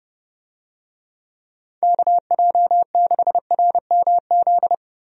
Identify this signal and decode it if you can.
Morse code, heard as a keyed tone, about 30 wpm, 700 Hz. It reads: KJ6RMZ
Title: USA Call Signs - 6 Characters - Repeated 3x - 30wpm